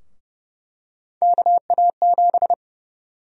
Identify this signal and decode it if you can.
Morse code, heard as a keyed tone, about 30 wpm, 700 Hz. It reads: KA7
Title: USA Call Sign Prefixes - 30wpm